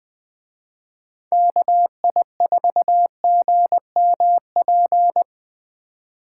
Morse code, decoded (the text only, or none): KI4GMP